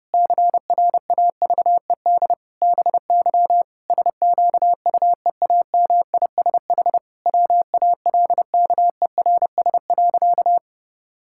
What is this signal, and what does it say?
Morse code, heard as a keyed tone, about 30 wpm, 700 Hz. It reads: CRAVED BY SQUEAMISH WALKERS.
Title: Pangram Challenge v2 - 30wpm